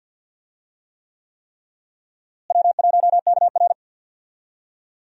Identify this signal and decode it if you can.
Morse code, heard as a keyed tone, about 52 wpm, 700 Hz. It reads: W1KR
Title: USA Call Signs - 4 Characters - Repeated 3x - 50wpm